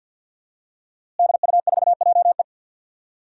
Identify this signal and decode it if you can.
Morse code, heard as a keyed tone, about 51 wpm, 700 Hz. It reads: DU4JE
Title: DXCC-100 Call Signs - 5 Characters - Repeated 3x - 50wpm